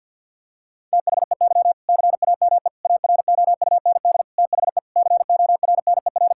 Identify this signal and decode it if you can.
Morse code, heard as a keyed tone, about 50 wpm, 700 Hz. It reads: THEY CAME AROUND THE CORNER
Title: Sentences from Top 1600 Words 50wpm